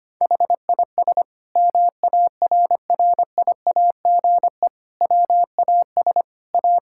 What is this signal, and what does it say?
Morse code, heard as a keyed tone, about 25 wpm, 700 Hz. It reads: HIS MARRIAGE WAS A
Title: Sentences from Top 1000 Words - Review Part 5 - 25wpm